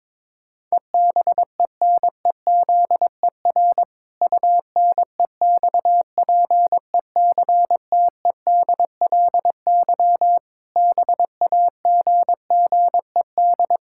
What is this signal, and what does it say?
Morse code, heard as a keyed tone, about 22 wpm, 700 Hz. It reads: EBENEZER UNEXPECTEDLY BAGGED
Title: Pangram Challenge v2 - 22wpm